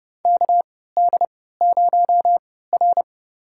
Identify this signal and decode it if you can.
Morse code, heard as a keyed tone, about 30 wpm, spaced as for 17 wpm, 700 Hz. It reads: KD0R
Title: USA Call Signs - 4 Characters - 3x Character Spacing - 30wpm